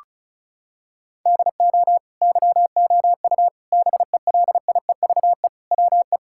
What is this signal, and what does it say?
Morse code, heard as a keyed tone, about 34 wpm, 700 Hz. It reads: DO YOU BELIEVE WE
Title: Sentences from Top 1000 Words - Review Part 6 - 35wpm